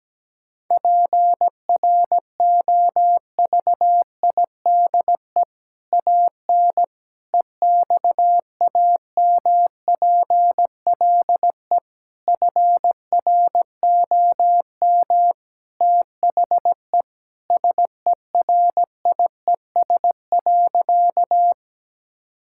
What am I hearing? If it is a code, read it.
Morse code, heard as a keyed tone, about 17 wpm, 700 Hz. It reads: PROVIDE AN EXAMPLE FROM THE SERIES.
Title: Sentences from Top 1000 Words - Review Part 5 - 17wpm